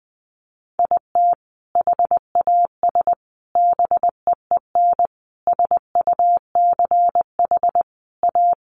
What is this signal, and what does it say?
Morse code, heard as a keyed tone, about 20 wpm, 700 Hz. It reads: IT HAS BEEN SUCH A
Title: Sentences from Top 1100 Words 20wpm